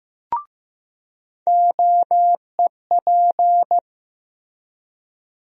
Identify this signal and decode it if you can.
Morse code, heard as a keyed tone, about 15 wpm, 700 Hz. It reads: OEP
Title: USA Call Sign Postfixes - 3 Letters 15wpm